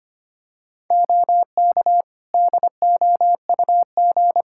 Morse code, heard as a keyed tone, about 25 wpm, 700 Hz. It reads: OK DOUG
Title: K1USN SST QSO Practice 25wpm